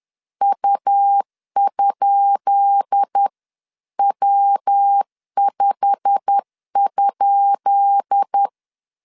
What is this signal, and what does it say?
Morse code, heard as a keyed tone, about 10 wpm, 800 Hz. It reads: U?W5?
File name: LEZIONI / 10 SIGNS / 10C DOT 10wpm 2940